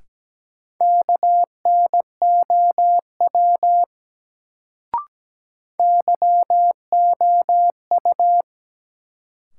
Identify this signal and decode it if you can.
Morse code, heard as a keyed tone, about 17 wpm, 700 Hz. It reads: KNOW YOU
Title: Sets of 2 Words - New Word Drill - 17wpm